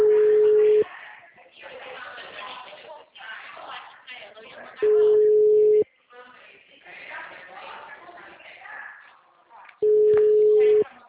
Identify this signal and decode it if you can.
Ringback tone (Europe, 425 Hz), with speech.
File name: out- -2024- - - .9139